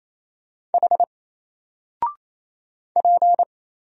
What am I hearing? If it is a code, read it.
Morse code, heard as a keyed tone, about 27 wpm, 700 Hz. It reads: H P
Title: Single Letter-Number 28wpm